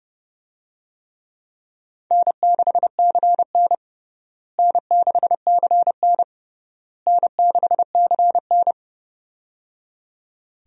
Morse code, heard as a keyed tone, about 31 wpm, 700 Hz. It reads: N6CN N6CN N6CN
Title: USA Call Signs - 4 Characters - Repeated 3x - 30wpm